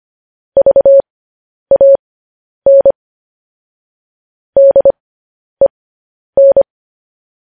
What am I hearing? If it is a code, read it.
Morse code, heard as a keyed tone, about 25 wpm, spaced as for 10 wpm, 550 Hz. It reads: VAN DEN
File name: podcast / News headlines-1x-25x10